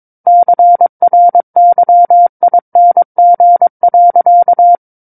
Morse code, heard as a keyed tone, about 22 wpm, 700 Hz. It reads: CRYING.
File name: Book One / 13-Chapter 13 - 22 WPM